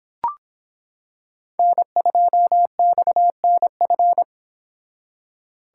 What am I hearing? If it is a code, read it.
Morse code, heard as a keyed tone, about 26 wpm, 700 Hz. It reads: N2XNF